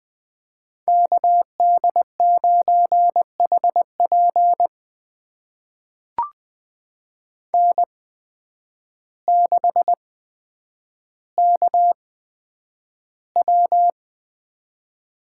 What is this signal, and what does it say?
Morse code, heard as a keyed tone, about 20 wpm, spaced as for 9 wpm, 700 Hz. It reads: KD9HP N6KW
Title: USA Call Signs - 5 Characters - 8x Character Spacing - 20wpm